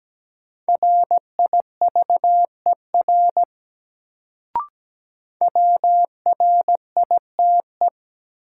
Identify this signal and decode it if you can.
Morse code, heard as a keyed tone, about 17 wpm, 700 Hz. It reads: RIVER WRITE
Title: Sentences from Top 300 Words - New Word Drill - 17wpm